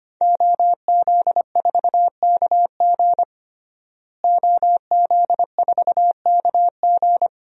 Morse code, heard as a keyed tone, about 25 wpm, 700 Hz. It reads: OZ4KG OZ4KG
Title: DXCC-100 Call Signs - 5 Characters - Repeated 3x - 25wpm